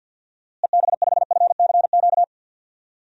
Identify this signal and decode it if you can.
Morse code, heard as a keyed tone, about 51 wpm, 700 Hz. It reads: EB5FCQ
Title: DXCC-100 Call Signs - 6 Characters - Repeated 2x - 50wpm